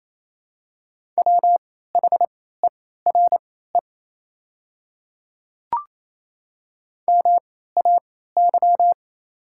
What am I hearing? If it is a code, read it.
Morse code, heard as a keyed tone, about 28 wpm, spaced as for 16 wpm, 690 Hz. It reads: WHERE MAY
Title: 3x Character Spacing - Top 200 Words - 28wpm